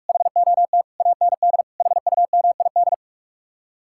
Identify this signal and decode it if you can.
Morse code, heard as a keyed tone, about 58 wpm, 700 Hz. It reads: HOT AND HUMID